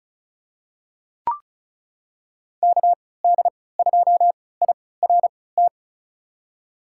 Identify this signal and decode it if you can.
Morse code, heard as a keyed tone, about 36 wpm, spaced as for 20 wpm, 700 Hz. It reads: KD2IRT
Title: USA Call Signs - 6 Characters - 3x Character Spacing - 35wpm